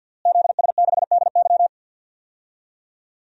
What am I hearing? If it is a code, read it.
Morse code, heard as a keyed tone, about 52 wpm, 700 Hz. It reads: ZS6DY
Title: DXCC-100 Call Signs - 5 Characters - Repeated 2x - 50wpm